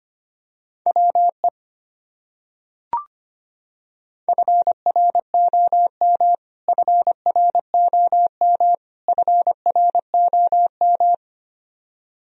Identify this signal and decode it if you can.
Morse code, heard as a keyed tone, about 25 wpm, 700 Hz. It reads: WE FROM FROM FROM
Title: Top 100 Words Repeated 25wpm